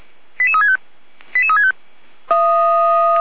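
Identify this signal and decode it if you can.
Five-tone selective call (ZVEI), heard as quick runs of five tones, tones of about 80 ms. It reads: 88365 88365
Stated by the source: Five-tone ZVEI selective call received on SDR.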